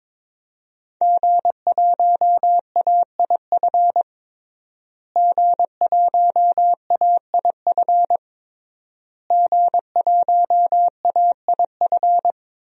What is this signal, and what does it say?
Morse code, heard as a keyed tone, about 22 wpm, 700 Hz. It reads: G1AIF G1AIF G1AIF
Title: DXCC-100 Call Signs - 5 Characters - Repeated 3x - 22wpm